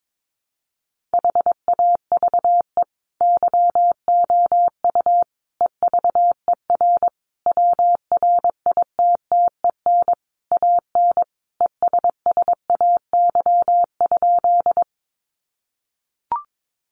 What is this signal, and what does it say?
Morse code, heard as a keyed tone, about 22 wpm, 700 Hz. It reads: HAVE YOU EVER WRITTEN AN ESSAY?
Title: Sentences from Top 1500 Words 22wpm